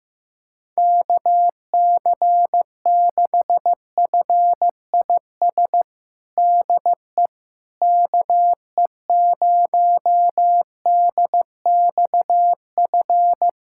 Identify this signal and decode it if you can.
Morse code, heard as a keyed tone, about 15 wpm, 700 Hz. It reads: KC6FIS DE KE0DXF
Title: USA Callsign DE Callsign - 6 Characters - 15wpm